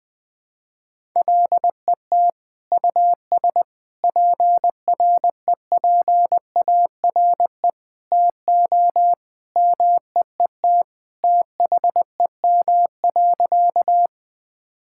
Morse code, heard as a keyed tone, about 20 wpm, 700 Hz. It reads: LET US PREPARE TO MEET THEM.
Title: Sentences from Top 1000 Words - Review Part 6 - 20wpm